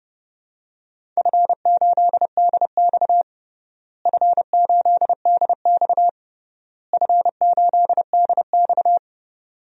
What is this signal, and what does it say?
Morse code, heard as a keyed tone, about 32 wpm, 700 Hz. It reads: F8DX F8DX F8DX
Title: DXCC-100 Call Signs - 4 Characters - Repeated 3x - 30wpm